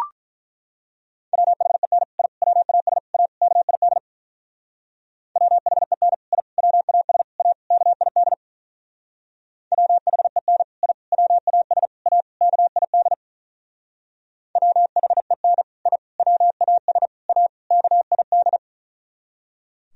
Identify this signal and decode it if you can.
Morse code, heard as a keyed tone, about 44 wpm, 700 Hz. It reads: WHEN I WAS A KID WHEN I WAS A KID WHEN I WAS A KID WHEN I WAS A KID
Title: Speed-Racing - Sets of 5 Words 35wpm